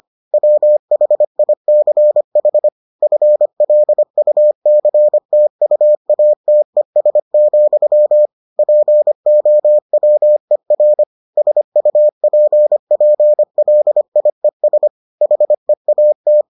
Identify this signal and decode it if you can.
Morse code, heard as a keyed tone, about 25 wpm, 600 Hz. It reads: WHICH FLUCTUATES, POWER SUPPLIES HEAT